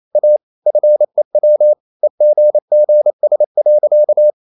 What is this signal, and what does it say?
Morse code, heard as a keyed tone, about 28 wpm, 600 Hz. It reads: A FEW EGGS.
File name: CW / Practice / EnglishSayings28